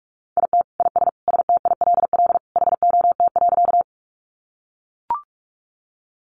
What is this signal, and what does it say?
Morse code, heard as a keyed tone, about 44 wpm, 700 Hz. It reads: IT IS STILL HOT.
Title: Additional Easy Phrases 45wpm